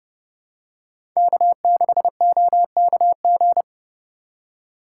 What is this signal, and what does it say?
Morse code, heard as a keyed tone, about 30 wpm, 700 Hz. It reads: K6OKG
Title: USA Call Signs - 5 Characters - 3x Character Spacing - 30wpm